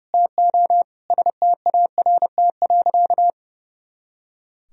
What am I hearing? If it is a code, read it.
Morse code, heard as a keyed tone, about 30 wpm, 700 Hz. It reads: TO START.
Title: Sentences from Top 300 Words and Abbreviations 30wpm